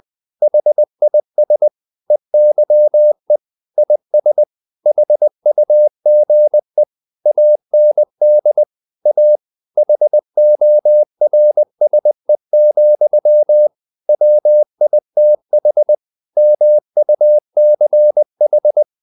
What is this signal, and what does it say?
Morse code, heard as a keyed tone, about 20 wpm, 600 Hz. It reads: HIS EYE IS HUGE AND A HORSE, WITH MUCH
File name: the old man and the sea / the old man and the sea part 1 20wpm